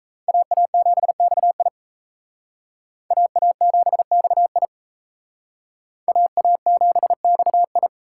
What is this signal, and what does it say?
Morse code, heard as a keyed tone, about 35 wpm, 700 Hz. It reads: AA7XI AA7XI AA7XI
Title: Speed Racing - USA 2x2 Call Signs 28wpm